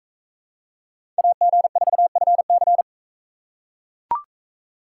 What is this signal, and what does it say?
Morse code, heard as a keyed tone, about 41 wpm, 700 Hz. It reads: AG4FC